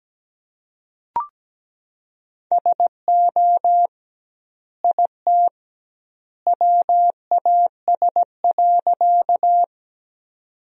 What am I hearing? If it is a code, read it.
Morse code, heard as a keyed tone, about 17 wpm, 700 Hz. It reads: SO IT WAS.
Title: 2x Word-Spacing - Sentences from Top 100 Words - 17wpm